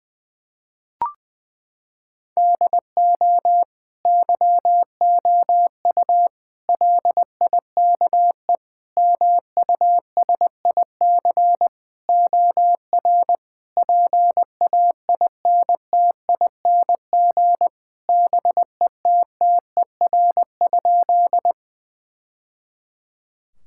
Morse code, heard as a keyed tone, about 20 wpm, 700 Hz. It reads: DO YOU LIKE MUSIC OR PAINTING BETTER?
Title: Sentences from Top 1000 Words - Review Part 6 - 20wpm